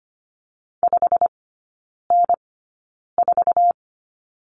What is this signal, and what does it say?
Morse code, heard as a keyed tone, about 25 wpm, spaced as for 9 wpm, 700 Hz. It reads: 5N4
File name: Lesson 17e - Letter u - Three Characters 25wpm